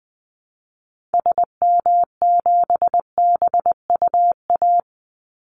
Morse code, heard as a keyed tone, about 20 wpm, 700 Hz. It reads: SM7BUA
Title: Call Signs Repeated 2x - 20wpm